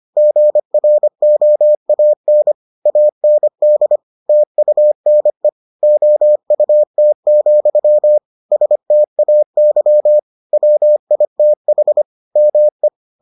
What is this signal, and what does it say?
Morse code, heard as a keyed tone, about 25 wpm, 600 Hz. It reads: GROAN AND TUNE OUT, STAY WITH ME